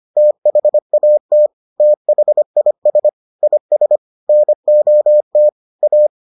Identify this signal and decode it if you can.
Morse code, heard as a keyed tone, about 25 wpm, 600 Hz. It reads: THAT THIS IS NOT A